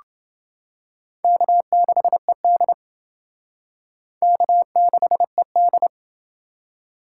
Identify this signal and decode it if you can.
Morse code, heard as a keyed tone, about 29 wpm, 700 Hz. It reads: K6ED K6ED